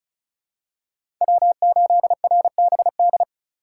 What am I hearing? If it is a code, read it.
Morse code, heard as a keyed tone, about 36 wpm, 700 Hz. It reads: W8RBD
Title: USA Call Signs - 5 Characters - 4x Character Spacing - 35wpm